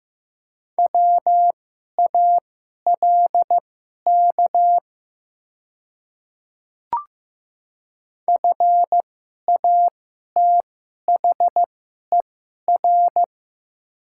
Morse code, heard as a keyed tone, about 15 wpm, spaced as for 11 wpm, 700 Hz. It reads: WALK FATHER
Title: 2x Character Spacing - Top 500 Words - 15wpm